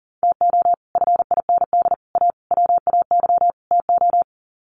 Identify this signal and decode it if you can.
Morse code, heard as a keyed tone, about 40 wpm, 700 Hz. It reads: TO FIND A WAY TO
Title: Sets of 5 Words 40wpm